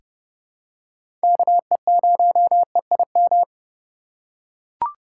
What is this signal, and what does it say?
Morse code, heard as a keyed tone, about 30 wpm, 700 Hz. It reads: KE0EIM
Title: USA Call Signs - 6 Characters - 2x Character Spacing - 30wpm